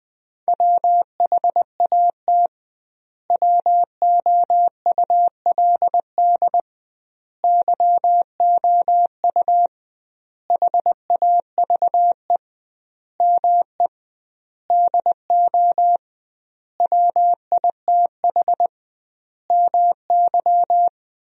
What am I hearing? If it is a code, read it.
Morse code, heard as a keyed tone, about 20 wpm, 700 Hz. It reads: WHAT WOULD YOU HAVE ME DO WITH MY
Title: 2x Word-Spacing - Sentences from Top 300 Words - 20wpm